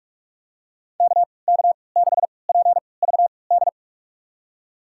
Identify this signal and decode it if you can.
Morse code, heard as a keyed tone, about 46 wpm, spaced as for 26 wpm, 700 Hz. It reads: KK6PVD
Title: USA Call Signs - 6 Characters - 3x Character Spacing - 45wpm